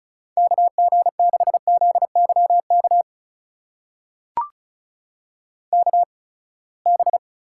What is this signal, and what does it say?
Morse code, heard as a keyed tone, about 36 wpm, spaced as for 24 wpm, 700 Hz. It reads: KG6ZYK KB
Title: USA Call Signs - 6 Characters - 8x Character Spacing - 35wpm